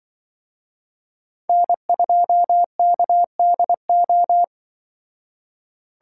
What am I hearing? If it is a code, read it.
Morse code, heard as a keyed tone, about 25 wpm, 700 Hz. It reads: N2KDO